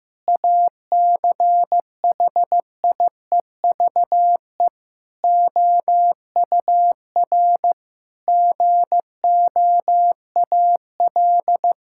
Morse code, heard as a keyed tone, about 15 wpm, 700 Hz. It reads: ACHIEVE OUR GOAL
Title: Sentences from Top 1000 Words - Review Part 6 - 15wpm